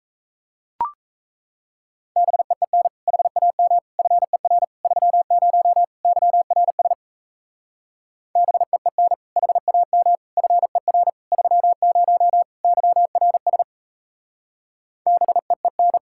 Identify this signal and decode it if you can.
Morse code, heard as a keyed tone, about 36 wpm, 700 Hz. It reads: BEEN HAM FER 30 YRS BEEN HAM FER 30 YRS BEEN